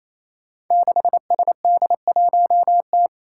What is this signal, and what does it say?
Morse code, heard as a keyed tone, about 29 wpm, 700 Hz. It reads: 6SD1T
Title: Five Challenging Letter-Number Combinations 28wpm